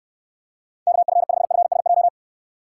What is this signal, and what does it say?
Morse code, heard as a keyed tone, about 69 wpm, 700 Hz. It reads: KF5FSJ